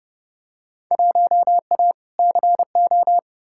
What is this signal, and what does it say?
Morse code, heard as a keyed tone, about 30 wpm, 700 Hz. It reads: 1A CO
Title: Field Day Exchanges - 30wpm